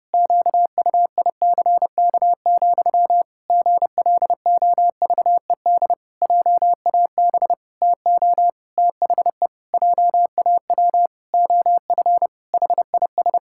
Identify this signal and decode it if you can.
Morse code, heard as a keyed tone, about 30 wpm, 700 Hz. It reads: QUICK, GLOVED JAB TO THE JAW OF HIS
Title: Pangram Challenge v2 - 30wpm